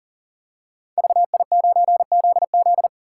Morse code, heard as a keyed tone, about 41 wpm, 700 Hz. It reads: VI9ZZ
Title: Code Groups - 5 Letter:Numbers - 40wpm